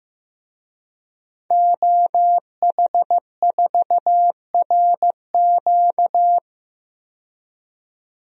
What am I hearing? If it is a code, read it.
Morse code, heard as a keyed tone, about 15 wpm, 700 Hz. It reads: OH4RQ
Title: DXCC-100 Call Signs - 5 Characters - 15wpm